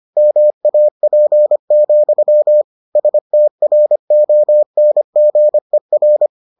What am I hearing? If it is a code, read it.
Morse code, heard as a keyed tone, about 25 wpm, 600 Hz. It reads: MAP, STRONGER